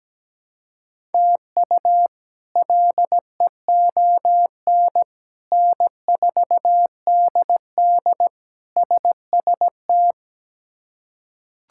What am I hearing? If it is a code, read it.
Morse code, heard as a keyed tone, about 17 wpm, 700 Hz. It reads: TU LEON N4DD SST
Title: K1USN SST QSO Practice - 17wpm